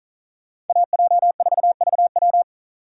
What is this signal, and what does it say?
Morse code, heard as a keyed tone, about 46 wpm, 700 Hz. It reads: AJ4VW